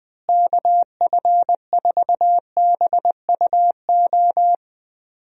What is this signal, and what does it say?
Morse code, heard as a keyed tone, about 20 wpm, 700 Hz. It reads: KF4BUO
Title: USA Call Signs - 6 Characters - Repeated 3x - 20wpm